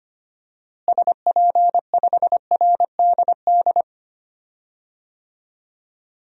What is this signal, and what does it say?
Morse code, heard as a keyed tone, about 25 wpm, 700 Hz. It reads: SP5RDD
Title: DXCC-100 Call Signs - 6 Characters - Repeated 2x - 25wpm